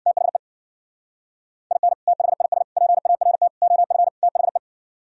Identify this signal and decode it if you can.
Morse code, heard as a keyed tone, about 64 wpm, 700 Hz. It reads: THE IN THIS PART OF THE